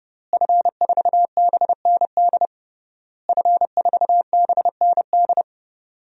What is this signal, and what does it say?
Morse code, heard as a keyed tone, about 30 wpm, 700 Hz. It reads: F4BND F4BND
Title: DXCC-100 Call Signs - 5 Characters - Repeated 3x - 30wpm